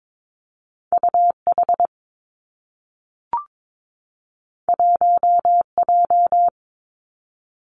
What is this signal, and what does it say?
Morse code, heard as a keyed tone, about 22 wpm, 700 Hz. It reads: U H 1 J
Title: Two Challenging Letter-Number Combinations 22wpm